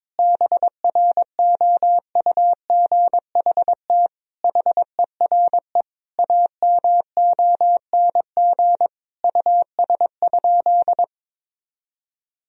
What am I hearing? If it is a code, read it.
Morse code, heard as a keyed tone, about 21 wpm, 700 Hz. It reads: BROUGHT HERE AMONG US?
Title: Sentences from Top 1000 Words - Review Part 3 - 22wpm